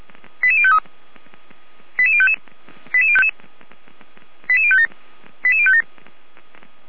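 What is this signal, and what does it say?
Five-tone selective call (ZVEI), heard as quick runs of five tones, tones of about 70 ms. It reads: 88053 88055 88057 88057
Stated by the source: ZVEI five-tone selective call received on SDR, from audio.f85360.feuerwehr-orlamuende.org (fire brigade Orlamünde).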